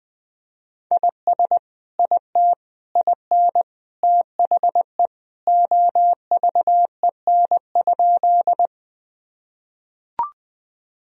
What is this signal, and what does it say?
Morse code, heard as a keyed tone, about 20 wpm, 700 Hz. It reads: IS IT IN THE OVEN?
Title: Sentences from Top 1000 Words - Review Part 4 - 20wpm